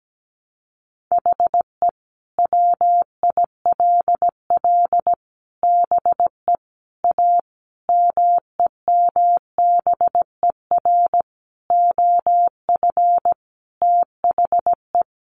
Morse code, heard as a keyed tone, about 17 wpm, 700 Hz. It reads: HE WILL BE A MEMBER OF THE
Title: Sentences from Top 1100 Words 17wpm